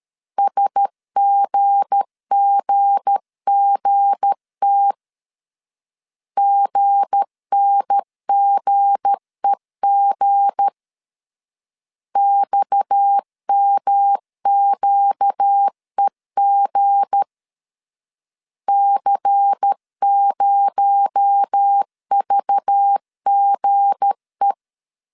Morse code, heard as a keyed tone, about 13 wpm, 800 Hz. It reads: SGGGT GNGEG XMQEG C0VGE
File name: LEZIONI / 7 GF9Z / 7 G 12wpm 2055